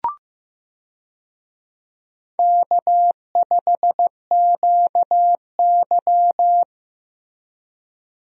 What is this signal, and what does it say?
Morse code, heard as a keyed tone, about 15 wpm, 700 Hz. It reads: K5QY
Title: USA Call Signs - 4 Characters - 15wpm